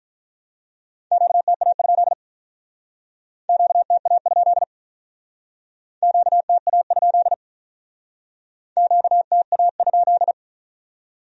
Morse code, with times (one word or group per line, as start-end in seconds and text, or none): 1.1-2.1 QTA?
3.5-4.6 QTA?
6.0-7.3 QTA?
8.8-10.3 QTA?